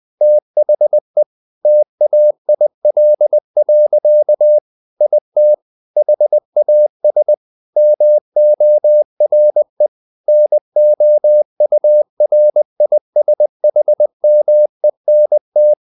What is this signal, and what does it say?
Morse code, heard as a keyed tone, about 20 wpm, 600 Hz. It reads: THE TAIL. IT HAS MORE NOURISHMENT
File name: the old man and the sea part 2 20wpm